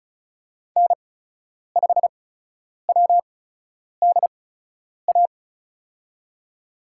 Morse code, heard as a keyed tone, about 36 wpm, spaced as for 10 wpm, 700 Hz. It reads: N5WDA
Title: USA Call Signs - 5 Characters - 8x Character Spacing - 35wpm